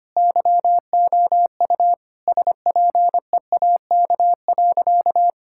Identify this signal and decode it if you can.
Morse code, heard as a keyed tone, about 25 wpm, 700 Hz. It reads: YOU SPEAK.
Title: Sentences from Top 1600 Words 25wpm